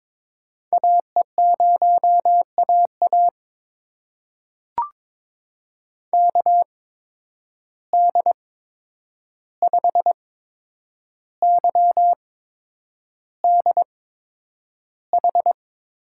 Morse code, heard as a keyed tone, about 22 wpm, spaced as for 9 wpm, 700 Hz. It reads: AE0AA KD5YDH
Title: USA Call Signs - 8x Character Spacing - 22wpm